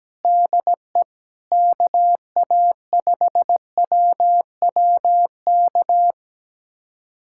Morse code, heard as a keyed tone, about 17 wpm, 700 Hz. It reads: DE KA5WWK
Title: USA Callsign DE Callsign 17wpm